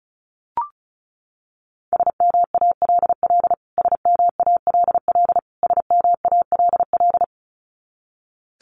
Morse code, heard as a keyed tone, about 34 wpm, 700 Hz. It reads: SMALL SMALL SMALL
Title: Top 500 Words Repeated 35wpm